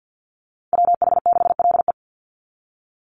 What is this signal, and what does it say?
Morse code, heard as a keyed tone, about 50 wpm, 700 Hz. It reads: WH6LE
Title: NA SOTA Chaser Callsigns 50wpm